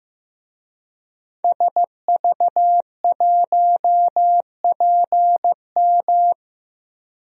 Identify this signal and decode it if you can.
Morse code, heard as a keyed tone, about 15 wpm, 700 Hz. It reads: SV1PM
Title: DXCC-100 Call Signs - 5 Characters - Repeated 3x - 15wpm